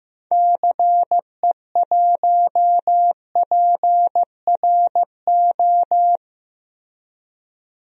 Morse code, heard as a keyed tone, about 15 wpm, 700 Hz. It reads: CE1PRO
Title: DXCC-100 Call Signs - 6 Characters - Repeated 2x - 15wpm